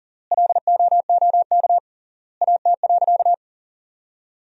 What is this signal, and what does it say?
Morse code, heard as a keyed tone, about 40 wpm, 700 Hz. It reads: LOOK AT.
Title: 3x Word-Spacing - Sentences from Top 200 Words - 40wpm